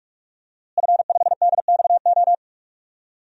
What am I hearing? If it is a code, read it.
Morse code, heard as a keyed tone, about 46 wpm, 700 Hz. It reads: F5DXO
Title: DXCC-100 Call Signs - 5 Characters - Repeated 3x - 45wpm